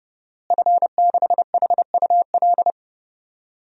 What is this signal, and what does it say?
Morse code, heard as a keyed tone, about 30 wpm, 700 Hz. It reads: F6HUL
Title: DXCC-100 Call Signs - 5 Characters - Repeated 2x - 30wpm